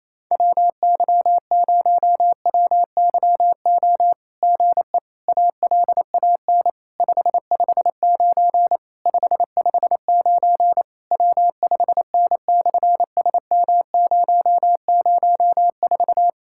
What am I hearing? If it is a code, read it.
Morse code, heard as a keyed tone, about 28 wpm, 700 Hz. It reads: WY0WYO GE ALAN 559 559 W5N/SM004
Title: SOTA QSO Practice 28wpm